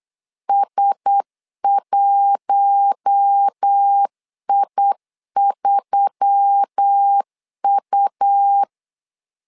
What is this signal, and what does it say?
Morse code, heard as a keyed tone, about 8 wpm, 800 Hz. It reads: S1I3U
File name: LEZIONI / 5 LBVC / 5 L 8wpm 2831